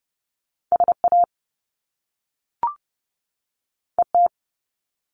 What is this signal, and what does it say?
Morse code, heard as a keyed tone, about 30 wpm, 700 Hz. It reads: SA ET
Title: Top Two Letter Combinations in Words 30wpm